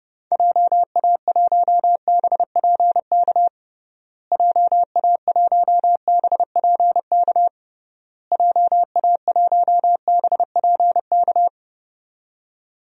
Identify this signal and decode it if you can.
Morse code, heard as a keyed tone, about 32 wpm, 700 Hz. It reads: JA1BPK JA1BPK JA1BPK
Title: DXCC-100 Call Signs - 6 Characters - Repeated 3x - 30wpm